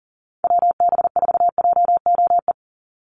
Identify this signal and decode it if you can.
Morse code, heard as a keyed tone, about 40 wpm, 700 Hz. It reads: WB4JOE
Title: USA Call Signs 40wpm